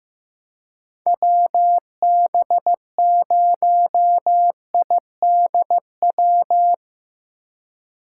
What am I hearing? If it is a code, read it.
Morse code, heard as a keyed tone, about 17 wpm, 700 Hz. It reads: WB0IDW
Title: Speed Racing - USA 2x3 Call Signs 15wpm